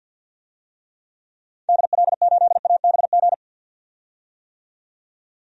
Morse code, heard as a keyed tone, about 46 wpm, 700 Hz. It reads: DL8ABG
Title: DXCC-100 Call Signs - 6 Characters - 50wpm